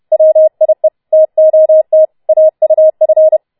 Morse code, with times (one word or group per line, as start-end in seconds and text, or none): 0.1-0.9 WIE
1.1-2.0 TOT
2.3-3.4 AUF